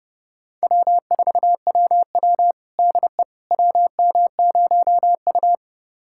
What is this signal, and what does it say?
Morse code, heard as a keyed tone, about 30 wpm, 700 Hz. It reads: W4WW DE WM0U
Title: USA Callsign DE Callsign - 4 Characters - 30wpm